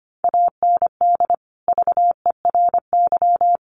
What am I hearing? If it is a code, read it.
Morse code, heard as a keyed tone, about 25 wpm, 700 Hz. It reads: AND VERY
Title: Sentences from Top 1300 Words 25wpm